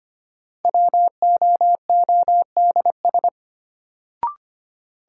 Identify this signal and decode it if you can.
Morse code, heard as a keyed tone, about 25 wpm, 700 Hz. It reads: WOODS
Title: Sentences from Top 1000 Words - Review Part 2 - New Word Drill - 25wpm